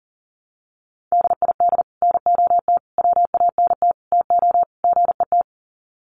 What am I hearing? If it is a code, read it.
Morse code, heard as a keyed tone, about 37 wpm, 700 Hz. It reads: DID NOT WANT TO GET
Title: Sets of 5 Words Encore2 40wpm